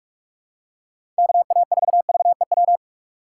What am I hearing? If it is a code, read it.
Morse code, heard as a keyed tone, about 46 wpm, 700 Hz. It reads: KA4VEW
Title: USA Call Signs - 6 Characters - 4x Character Spacing - 45wpm